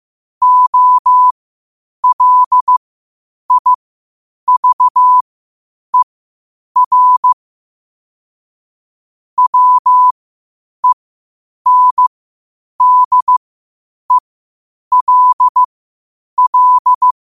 Morse code, heard as a keyed tone, about 15 wpm, spaced as for 8 wpm, 1000 Hz. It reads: OLIVER WENDELL
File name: qotdfiles / 8wpm